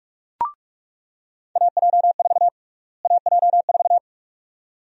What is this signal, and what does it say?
Morse code, heard as a keyed tone, about 48 wpm, 700 Hz. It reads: AJ4 AJ4
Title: USA Call Sign Prefixes - Repeated 2x - 45wpm